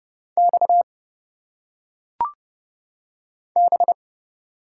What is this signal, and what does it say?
Morse code, heard as a keyed tone, about 30 wpm, 700 Hz. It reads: X B
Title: Single Letters - Phonetic Pronunciation 30wpm